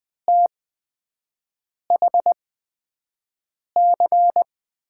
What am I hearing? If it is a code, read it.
Morse code, heard as a keyed tone, about 20 wpm, spaced as for 5 wpm, 700 Hz. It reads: THC
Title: USA Call Signs - 5 Characters - 8x Character Spacing - 20wpm